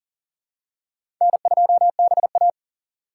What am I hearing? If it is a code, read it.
Morse code, heard as a keyed tone, about 40 wpm, 700 Hz. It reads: N2BA
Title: USA Call Signs - 4 Characters - 3x Character Spacing - 40wpm